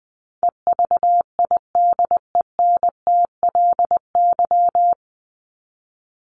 Morse code, heard as a keyed tone, about 20 wpm, 700 Hz. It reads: EVIDENTLY
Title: LY Suffix - Top 100 Words with Base Words 20wpm